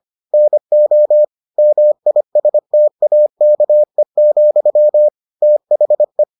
Morse code, heard as a keyed tone, about 25 wpm, 600 Hz. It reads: NO MISTAKE, THE